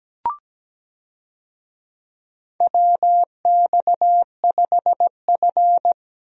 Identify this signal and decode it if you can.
Morse code, heard as a keyed tone, about 17 wpm, 700 Hz. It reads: WX5F
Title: USA Call Signs - 4 Characters - Repeated 3x - 17wpm